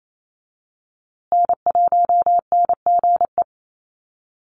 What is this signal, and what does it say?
Morse code, heard as a keyed tone, about 28 wpm, 700 Hz. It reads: N1NGE
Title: USA Call Signs Repeated 3x 28wpm